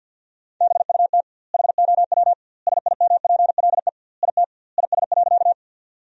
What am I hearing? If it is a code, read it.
Morse code, heard as a keyed tone, about 50 wpm, 700 Hz. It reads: BUT HOW SIMPLE IT IS.
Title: 2x Word-Spacing - Sentences from Top 400 Words - 50wpm